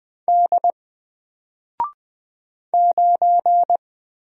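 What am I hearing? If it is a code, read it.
Morse code, heard as a keyed tone, about 20 wpm, 700 Hz. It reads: D 9
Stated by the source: Single letter and number Morse drill at 20 wpm, from N4LCD.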